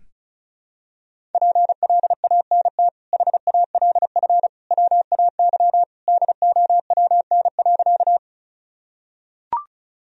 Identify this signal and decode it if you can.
Morse code, heard as a keyed tone, about 36 wpm, 700 Hz. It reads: PLANT HALF WAY DOWN.
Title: Sentences from Top 1000 Words - Review Part 1 - 35wpm